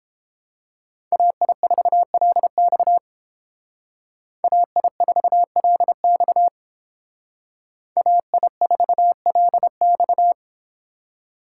Morse code, heard as a keyed tone, about 30 wpm, 700 Hz. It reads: AI4LX AI4LX AI4LX